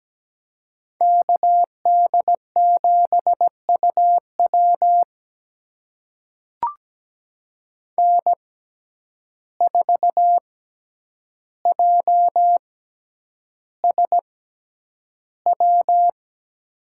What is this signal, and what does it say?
Morse code, heard as a keyed tone, about 17 wpm, spaced as for 9 wpm, 700 Hz. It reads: KD7UW N4JSW
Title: USA Call Signs - 5 Characters - 6x Character Spacing - 17wpm